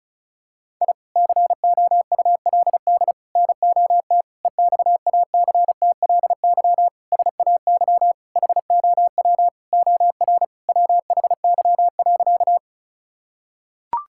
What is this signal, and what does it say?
Morse code, heard as a keyed tone, about 35 wpm, 700 Hz. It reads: I COULD NOT EXACTLY SAY HOW OR WHY.
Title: Sentences from Top 1000 Words - Review Part 6 - 35wpm